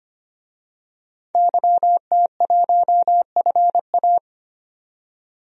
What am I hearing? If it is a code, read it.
Morse code, heard as a keyed tone, about 25 wpm, 700 Hz. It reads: YT1FA
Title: DXCC-100 Call Signs - 5 Characters - 25wpm